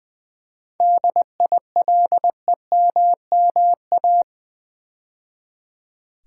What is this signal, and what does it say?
Morse code, heard as a keyed tone, about 20 wpm, 700 Hz. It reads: DILEMMA
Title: 7 Letter Words - First 3 Letters Given - 20wpm